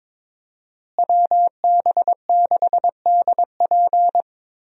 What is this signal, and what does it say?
Morse code, heard as a keyed tone, about 22 wpm, 700 Hz. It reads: WB6DP
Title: USA Call Signs - 5 Characters - Repeated 3x - 22wpm